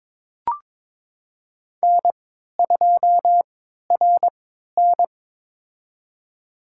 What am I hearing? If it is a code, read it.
Morse code, heard as a keyed tone, about 22 wpm, spaced as for 13 wpm, 700 Hz. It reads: N2RN
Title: USA Call Signs - 4 Characters - 3x Character Spacing - 22wpm